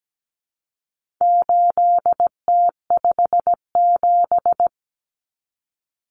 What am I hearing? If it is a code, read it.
Morse code, heard as a keyed tone, about 17 wpm, 700 Hz. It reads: 8T57
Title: Common Cut Numbers - 4-Digit 17wpm